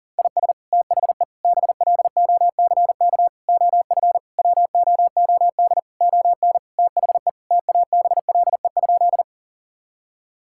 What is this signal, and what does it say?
Morse code, heard as a keyed tone, about 40 wpm, 700 Hz. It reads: IS THE BLOCK OF WOOD ON THE TABLE?
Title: Sentences from Top 1600 Words 40wpm